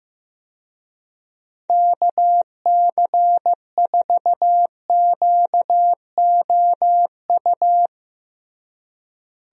Morse code, heard as a keyed tone, about 16 wpm, 700 Hz. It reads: KC4QOU
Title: USA Call Signs - 6 Characters - Repeated 2x - 15wpm